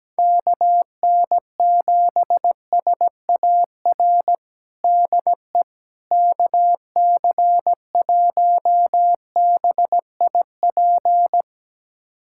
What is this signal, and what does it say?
Morse code, heard as a keyed tone, about 17 wpm, 700 Hz. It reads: KN7SAR DE KC1BIP
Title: USA Callsign DE Callsign 17wpm